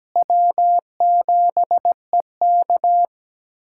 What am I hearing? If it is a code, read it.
Morse code, heard as a keyed tone, about 17 wpm, 700 Hz. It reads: W7EK
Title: USA Callsign DE Callsign - 4 Characters - 17wpm